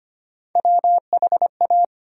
Morse code, heard as a keyed tone, about 25 wpm, 700 Hz. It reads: WHA
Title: USA Call Sign Postfixes - 3 Letters 25wpm